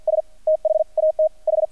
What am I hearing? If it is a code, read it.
Morse code, heard as a keyed tone, about 50 wpm, spaced as for 42 wpm, 630 Hz. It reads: S THAT H
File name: archived / CRSnet